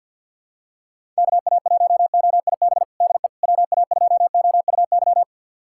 Morse code, heard as a keyed tone, about 51 wpm, 700 Hz. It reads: KA1OIB DE WA2OUY
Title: USA Callsign DE Callsign 50wpm